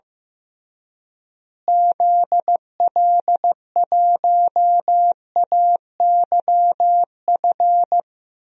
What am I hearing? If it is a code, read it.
Morse code, heard as a keyed tone, about 15 wpm, 700 Hz. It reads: ZL1AYF
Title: DXCC-100 Call Signs - 6 Characters - Repeated 3x - 15wpm